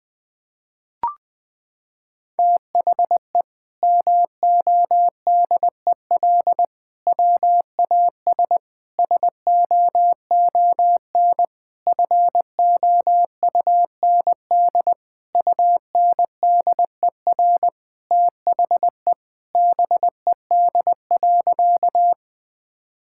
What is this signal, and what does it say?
Morse code, heard as a keyed tone, about 20 wpm, 700 Hz. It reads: THE MODEL WAS SOON FOUND UNDER THE BED.
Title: Sentences from Top 1000 Words - Review Part 5 - 20wpm